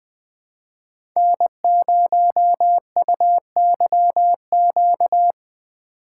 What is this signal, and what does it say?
Morse code, heard as a keyed tone, about 20 wpm, 700 Hz. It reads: N0UYQ
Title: USA Call Signs - 4x Character Spacing - 20wpm